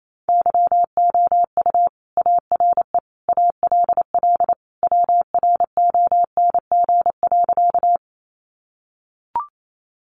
Morse code, heard as a keyed tone, about 28 wpm, 700 Hz. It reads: YOU ARE ALL WRONG.
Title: Sentences from Top 900 Words 28wpm